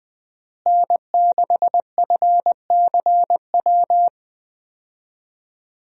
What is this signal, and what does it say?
Morse code, heard as a keyed tone, about 21 wpm, 700 Hz. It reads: N6FCW
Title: USA Call Signs - 5 Characters - Repeated 3x - 20wpm